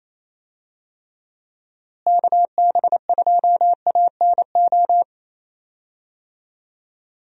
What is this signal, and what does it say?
Morse code, heard as a keyed tone, about 28 wpm, 700 Hz. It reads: KB2ANO
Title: USA Call Signs - 6 Characters - Repeated 3x - 28wpm